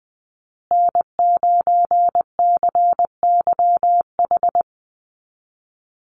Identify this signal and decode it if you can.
Morse code, heard as a keyed tone, about 20 wpm, 700 Hz. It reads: N9CYH
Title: USA Call Signs Repeated 3x 20wpm